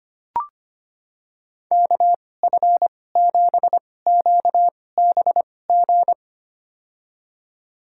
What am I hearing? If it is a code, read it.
Morse code, heard as a keyed tone, about 25 wpm, spaced as for 18 wpm, 700 Hz. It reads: KF7QBG
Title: USA Call Signs - 6 Characters - 2x Character Spacing - 25wpm